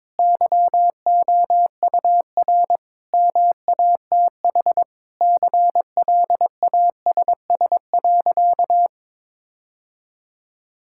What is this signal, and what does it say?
Morse code, heard as a keyed tone, about 22 wpm, 700 Hz. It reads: YOUR MATH CLASS.
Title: Sentences from Top 1000 Words - Review Part 6 - 22wpm